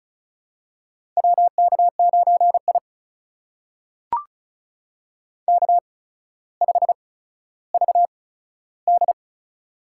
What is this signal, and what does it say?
Morse code, heard as a keyed tone, about 36 wpm, spaced as for 15 wpm, 700 Hz. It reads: WK9I K5VD
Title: USA Call Signs - 4 Characters - 8x Character Spacing - 35wpm